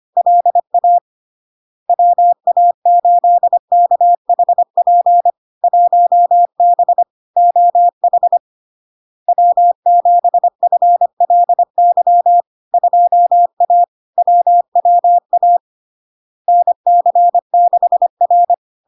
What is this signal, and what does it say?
Morse code, heard as a keyed tone, about 25 wpm, 700 Hz. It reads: LA WA8KHP 1B OH W7FLY 2A WWA NC6R
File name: LARK-FD-2022-Log-in-CW-25wpm-700-Hz